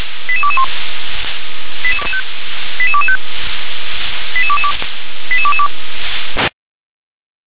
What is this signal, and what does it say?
Five-tone selective call (ZVEI), heard as quick runs of five tones, tones of about 80 ms. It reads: 88221 88225 88222 88222
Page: SDR/ZVEI